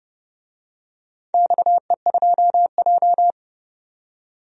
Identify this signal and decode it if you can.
Morse code, heard as a keyed tone, about 30 wpm, 700 Hz. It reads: XE2J
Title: DXCC-100 Call Signs - 4 Characters - 30wpm